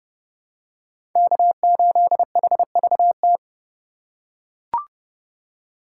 Morse code, heard as a keyed tone, about 31 wpm, 700 Hz. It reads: K8HVT